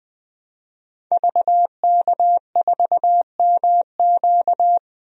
Morse code, heard as a keyed tone, about 20 wpm, 700 Hz. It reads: VK4MQ
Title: DXCC-100 Call Signs - 5 Characters - 20wpm